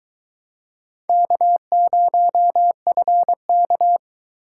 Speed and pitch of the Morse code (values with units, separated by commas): 21 wpm, 700 Hz